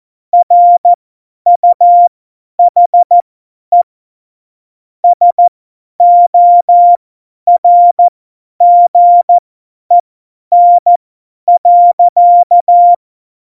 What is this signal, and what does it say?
Morse code, heard as a keyed tone, about 14 wpm, spaced as for 10 wpm, 700 Hz. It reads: RUHE SORGEN.
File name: Morsekurs / Witze / Witz 05. Januar 2021, 50 F70